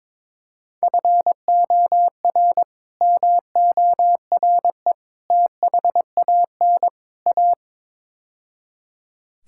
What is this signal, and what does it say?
Morse code, heard as a keyed tone, about 22 wpm, 700 Hz. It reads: FOR MORE THAN A
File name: practice / 22wpm / Build A Phrase - 5-Word Phrases - 22wpm